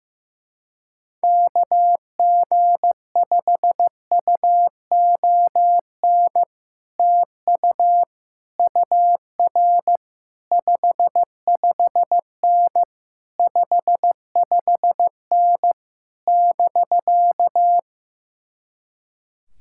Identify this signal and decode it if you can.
Morse code, heard as a keyed tone, about 15 wpm, 700 Hz. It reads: KG5UON TU UR 55N 55N <BK>
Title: Speed-Racing Common QSO Phrases Encore - 15wpm